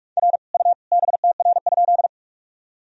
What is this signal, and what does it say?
Morse code, heard as a keyed tone, about 48 wpm, 700 Hz. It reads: R U BTR?
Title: Sentences from Top 400 Words and Abbreviations 45wpm